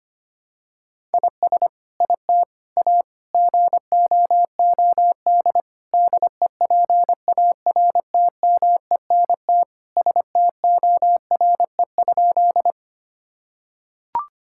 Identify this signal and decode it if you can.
Morse code, heard as a keyed tone, about 25 wpm, 700 Hz. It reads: IS IT A GOOD DEPARTMENT STORE?
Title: Sentences from Top 1000 Words - Review Part 5 - 25wpm